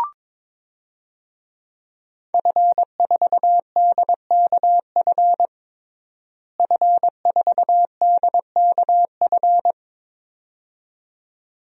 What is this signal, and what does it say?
Morse code, heard as a keyed tone, about 22 wpm, 700 Hz. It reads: F4DKF F4DKF
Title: DXCC-100 Call Signs - 5 Characters - Repeated 2x - 22wpm